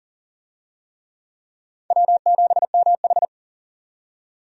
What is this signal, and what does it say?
Morse code, heard as a keyed tone, about 40 wpm, 700 Hz. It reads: W7MH
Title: USA Call Signs - 4 Characters - 40wpm